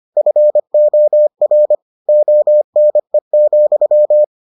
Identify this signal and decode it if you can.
Morse code, heard as a keyed tone, about 25 wpm, 600 Hz. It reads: FOR ONE,